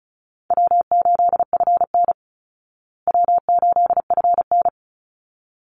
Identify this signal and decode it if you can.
Morse code, heard as a keyed tone, about 34 wpm, 700 Hz. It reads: W8FN W8FN
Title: Easy Call Signs Repeated 2x - 35wpm